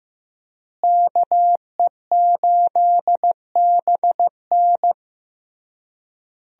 Morse code, heard as a keyed tone, about 15 wpm, 700 Hz. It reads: KE8BN
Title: USA Call Signs - 5 Characters - Repeated 3x - 15wpm